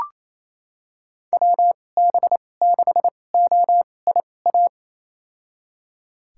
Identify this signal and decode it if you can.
Morse code, heard as a keyed tone, about 28 wpm, spaced as for 20 wpm, 700 Hz. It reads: WB6OIA
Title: USA Call Signs - 6 Characters - 2x Character Spacing - 28wpm